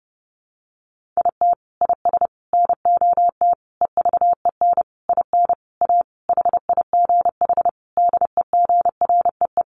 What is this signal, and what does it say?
Morse code, heard as a keyed tone, about 30 wpm, 700 Hz. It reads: IT IS NOT EVEN IN A HIGH DEGREE
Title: Sentences from Top 1500 Words 30wpm